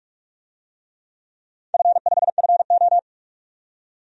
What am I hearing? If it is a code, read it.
Morse code, heard as a keyed tone, about 47 wpm, 700 Hz. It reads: F5FO
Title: DXCC-100 Call Signs - 4 Characters - 45wpm